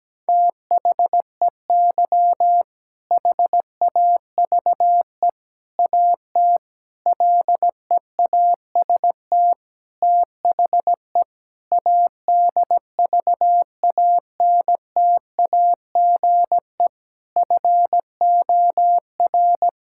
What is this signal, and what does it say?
Morse code, heard as a keyed tone, about 17 wpm, 700 Hz. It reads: THEY HAVE AT LEAST THE ADVANTAGE FOR
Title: Sentences from Top 1000 Words - Review Part 6 - 17wpm